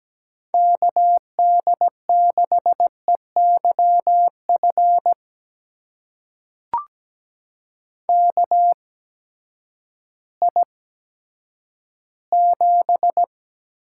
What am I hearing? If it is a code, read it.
Morse code, heard as a keyed tone, about 17 wpm, spaced as for 10 wpm, 700 Hz. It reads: KD6EYF KI7
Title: USA Call Signs - 6 Characters - 8x Character Spacing - 17wpm